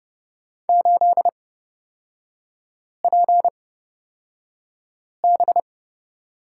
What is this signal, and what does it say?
Morse code, heard as a keyed tone, about 30 wpm, spaced as for 5 wpm, 700 Hz. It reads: 8PB
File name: Lesson 37e - Letter x - Three Characters 30wpm